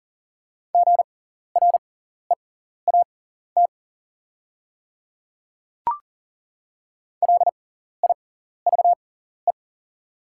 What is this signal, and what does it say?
Morse code, heard as a keyed tone, about 40 wpm, spaced as for 14 wpm, 690 Hz. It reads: GREAT LIVE
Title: 6x Character Spacing - Top 300 Words - 40wpm